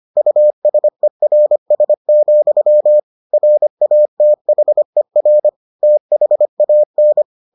Morse code, heard as a keyed tone, about 25 wpm, 600 Hz. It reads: USERS, RATHER THAN